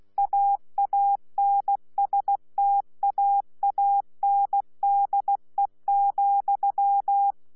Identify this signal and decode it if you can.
Morse code, heard as a keyed tone, about 16 wpm, 800 Hz. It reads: AANSTAANDE,
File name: Z PI4VRZ / PI4VRZ 14-jun-2025